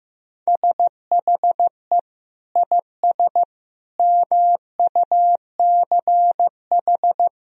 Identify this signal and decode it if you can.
Morse code, heard as a keyed tone, about 15 wpm, 700 Hz. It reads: SHE IS MUCH
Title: Sentences from Top 400 Words and Abbreviations 15wpm